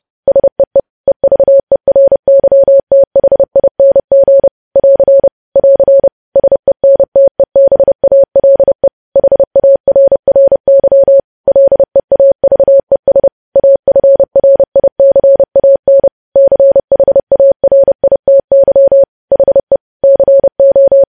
Morse code, heard as a keyed tone, about 30 wpm, 550 Hz. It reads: SEE EVERYTHING <AR> <AR> SENTEBALE HARRY LEAVES AFRICAN CHARITY HE CO
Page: news headlines in morse code at 30 wpm